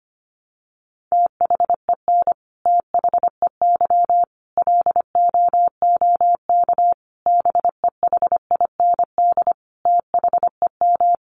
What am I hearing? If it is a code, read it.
Morse code, heard as a keyed tone, about 25 wpm, 700 Hz. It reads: THEN THEY LOOK BEHIND THEM
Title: Sentences from Top 1300 Words 25wpm